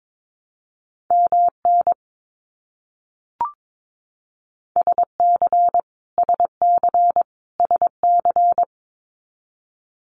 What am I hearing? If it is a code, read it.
Morse code, heard as a keyed tone, about 22 wpm, 700 Hz. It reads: MN SC SC SC
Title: US States Repeated 3x 22wpm